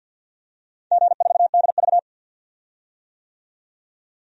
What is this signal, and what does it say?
Morse code, heard as a keyed tone, about 51 wpm, 700 Hz. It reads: G4DV
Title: DXCC-100 Call Signs - 4 Characters - 50wpm